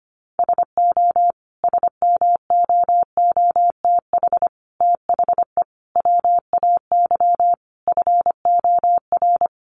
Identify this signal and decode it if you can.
Morse code, heard as a keyed tone, about 25 wpm, 700 Hz. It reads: SO SMOOTH THE WAY FOR
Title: Sentences from Top 1400 Words 25wpm